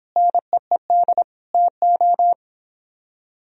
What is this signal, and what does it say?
Morse code, heard as a keyed tone, about 26 wpm, 700 Hz. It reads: NEED TO